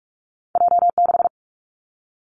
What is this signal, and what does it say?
Morse code, heard as a keyed tone, about 46 wpm, 700 Hz. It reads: J 6
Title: Two Challenging Letter-Number Combinations 45wpm